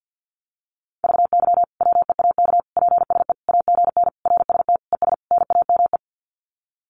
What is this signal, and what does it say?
Morse code, heard as a keyed tone, about 50 wpm, 700 Hz. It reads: VY WEAK PSE AGN RST ES NAME
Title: Common QSO Phrases 50wpm